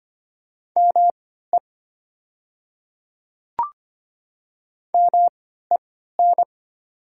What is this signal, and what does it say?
Morse code, heard as a keyed tone, about 25 wpm, spaced as for 14 wpm, 700 Hz. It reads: ME MEN
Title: 3x Character Spacing - Top 200 Words - 25wpm